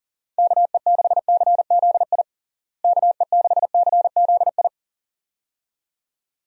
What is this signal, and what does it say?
Morse code, heard as a keyed tone, about 37 wpm, 700 Hz. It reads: KE6CZI KE6CZI
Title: USA Call Signs - 6 Characters - Repeated 2x - 40wpm